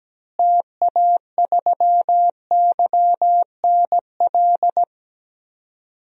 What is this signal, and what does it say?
Morse code, heard as a keyed tone, about 17 wpm, 700 Hz. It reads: TA3YNL
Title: DXCC-100 Call Signs - 6 Characters - Repeated 2x - 17wpm